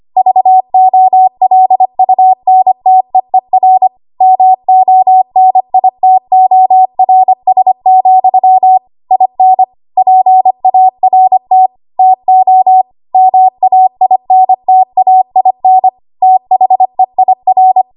Morse code, heard as a keyed tone, about 25 wpm, 750 Hz. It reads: VOLUNTEER MONITORS, IN PART TO MAINTAIN THEIR
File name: Archive / 25 WPM / 25WPM